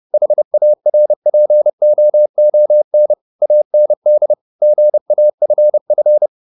Morse code, heard as a keyed tone, about 30 wpm, 600 Hz. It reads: HARPOON AND GAFF
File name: the old man and the sea / the old man and the sea 1 30wpm